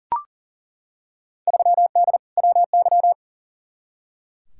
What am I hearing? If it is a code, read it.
Morse code, heard as a keyed tone, about 40 wpm, 700 Hz. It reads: 3D WY
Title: Field Day Exchanges Encore - 40wpm